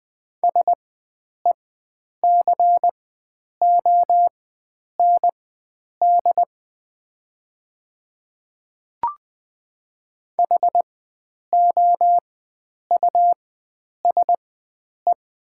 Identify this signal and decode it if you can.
Morse code, heard as a keyed tone, about 20 wpm, spaced as for 9 wpm, 700 Hz. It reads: SECOND HOUSE
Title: 4x Character Spacing - Top 300 Words - 20wpm